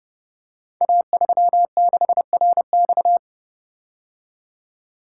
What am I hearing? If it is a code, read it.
Morse code, heard as a keyed tone, about 30 wpm, 700 Hz. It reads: A36RX
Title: Code Groups - 5 Letter:Numbers - 30wpm